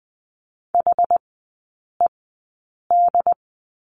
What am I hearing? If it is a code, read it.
Morse code, heard as a keyed tone, about 20 wpm, spaced as for 8 wpm, 700 Hz. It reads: HED
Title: 067 Lesson 12e - Letter d - Three Characters 20wpm